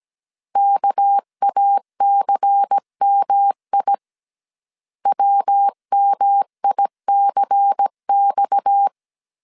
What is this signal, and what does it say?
Morse code, heard as a keyed tone, about 17 wpm, 800 Hz. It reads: KACMI WMICX
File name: 16wpm